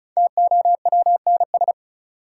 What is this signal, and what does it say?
Morse code, heard as a keyed tone, about 34 wpm, 700 Hz. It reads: TOWNS
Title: Sentences from Top 1000 Words - Review Part 6 - New Word Drill - 35wpm